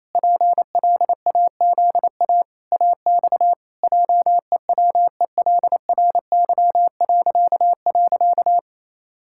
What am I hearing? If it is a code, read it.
Morse code, heard as a keyed tone, about 27 wpm, 700 Hz. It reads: PLAZA AX JEWELRY..
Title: Pangram Challenge v2 - 28wpm